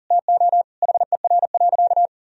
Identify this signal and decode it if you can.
Morse code, heard as a keyed tone, about 40 wpm, 700 Hz. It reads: TO HER.
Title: Sentences from Top 1000 Words - Review Part 3 - 40wpm